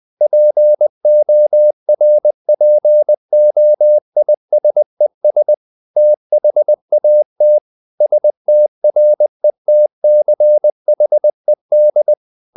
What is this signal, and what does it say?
Morse code, heard as a keyed tone, about 20 wpm, 600 Hz. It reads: PORPOISES THAT STRETCHED
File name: the old man and the sea part 2 20wpm